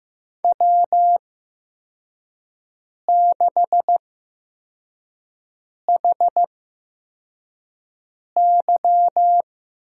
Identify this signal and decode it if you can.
Morse code, heard as a keyed tone, about 15 wpm, spaced as for 4 wpm, 700 Hz. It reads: W6HY
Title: USA Call Signs - 8x Character Spacing - 15wpm